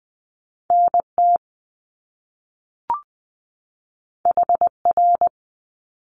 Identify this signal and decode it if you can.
Morse code, heard as a keyed tone, about 20 wpm, 700 Hz. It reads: NT HR
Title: Top Two Letter Combinations in Words 20wpm